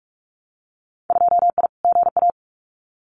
Morse code, heard as a keyed tone, about 44 wpm, 700 Hz. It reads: 2I GA
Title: Winter Field Day Exchanges 45wpm